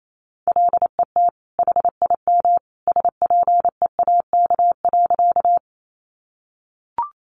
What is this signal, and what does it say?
Morse code, heard as a keyed tone, about 28 wpm, 700 Hz. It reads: LET HIM SPEAK.
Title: Sentences from Top 1100 Words 28wpm